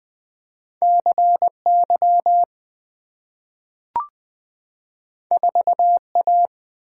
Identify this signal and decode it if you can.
Morse code, heard as a keyed tone, about 20 wpm, 700 Hz. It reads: CY 4A
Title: Two Letter-Number Combinations - 20wpm